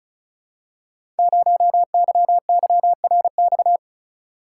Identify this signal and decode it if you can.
Morse code, heard as a keyed tone, about 36 wpm, 690 Hz. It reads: 0YYRX
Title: Code Groups - 5 Letter:Numbers - 35wpm